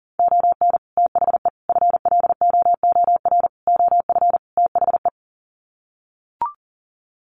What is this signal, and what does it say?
Morse code, heard as a keyed tone, about 40 wpm, 700 Hz. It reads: ON THE FLOOR OF THE
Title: Sets of 5 Words 40wpm